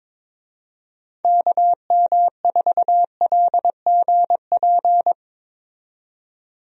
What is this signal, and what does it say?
Morse code, heard as a keyed tone, about 22 wpm, 700 Hz. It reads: KM4LGP
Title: USA Call Signs - 6 Characters - 22wpm